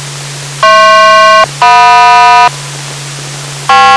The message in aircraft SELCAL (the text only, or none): HP-JM